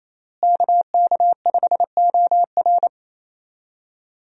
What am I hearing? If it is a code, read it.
Morse code, heard as a keyed tone, about 28 wpm, 700 Hz. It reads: KK5OR
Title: USA Call Signs - 5 Characters - Repeated 3x - 28wpm